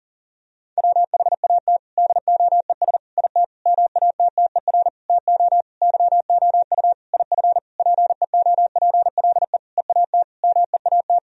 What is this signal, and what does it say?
Morse code, heard as a keyed tone, about 40 wpm, 700 Hz. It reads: WHAT DOES IT MATTER TO YOU IF PEOPLE EAT MEAT
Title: Sentences from Top 1000 Words - Review Part 3 - 40wpm